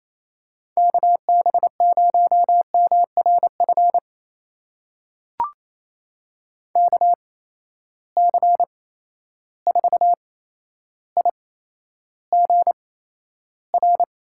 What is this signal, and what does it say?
Morse code, heard as a keyed tone, about 28 wpm, spaced as for 12 wpm, 700 Hz. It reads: KB0MRF KC4IGR
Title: USA Call Signs - 6 Characters - 8x Character Spacing - 28wpm